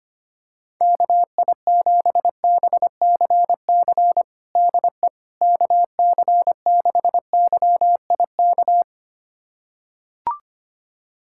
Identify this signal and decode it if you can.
Morse code, heard as a keyed tone, about 25 wpm, 700 Hz. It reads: KI7BCC DE KC6YIK
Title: USA Callsign DE Callsign 25wpm